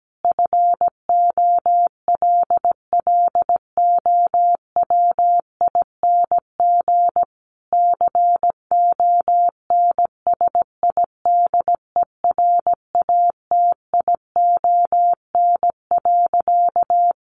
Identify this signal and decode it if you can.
Morse code, heard as a keyed tone, about 17 wpm, 700 Hz. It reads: FOLLOWING CONSIDERATION.
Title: Sentences from Top 1500 Words 17wpm